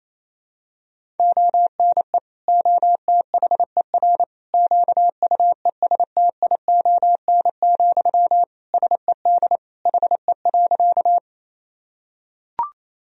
Morse code, heard as a keyed tone, about 28 wpm, 700 Hz. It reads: ONE OTHER QUESTION, SED HE.
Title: Sentences from Top 400 Words and Abbreviations 28wpm